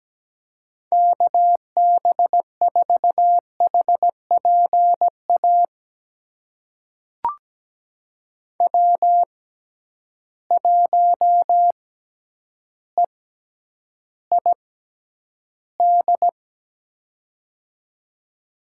Morse code, heard as a keyed tone, about 17 wpm, spaced as for 9 wpm, 700 Hz. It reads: KB4HPA W1EID
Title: USA Call Signs - 6x Character Spacing - 17wpm